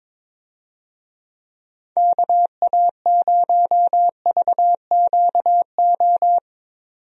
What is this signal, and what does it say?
Morse code, heard as a keyed tone, about 22 wpm, 700 Hz. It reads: KA0VQO
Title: USA Call Signs - 6 Characters - 22wpm